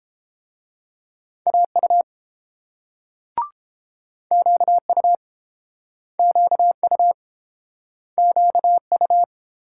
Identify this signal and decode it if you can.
Morse code, heard as a keyed tone, about 30 wpm, 700 Hz. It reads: AU QU QU QU